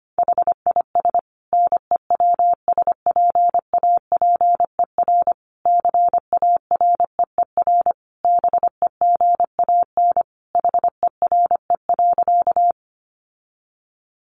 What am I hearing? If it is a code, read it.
Morse code, heard as a keyed tone, about 25 wpm, 700 Hz. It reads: HIS NEWSPAPER CAREER BEGAN HERE.
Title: Sentences from Top 1100 Words 25wpm